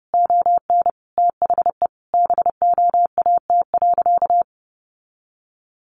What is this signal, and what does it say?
Morse code, heard as a keyed tone, about 30 wpm, 700 Hz. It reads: ON THE BOAT.
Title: Sentences from Top 1400 Words 30wpm